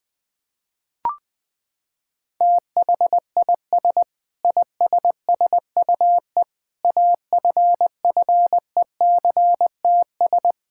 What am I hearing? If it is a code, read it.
Morse code, heard as a keyed tone, about 20 wpm, 700 Hz. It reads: THIS ISSUE AFFECTS
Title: Sentences with 7-Word Phrases - 20wpm